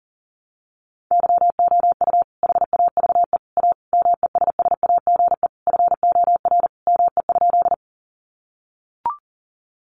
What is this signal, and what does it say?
Morse code, heard as a keyed tone, about 40 wpm, 700 Hz. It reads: YOU HAVE A MESSAGE FOR ME?
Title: Sentences from Top 1100 Words 40wpm